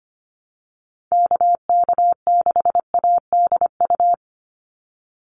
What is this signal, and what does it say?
Morse code, heard as a keyed tone, about 25 wpm, 700 Hz. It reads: KK6ADU
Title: USA Call Signs 25wpm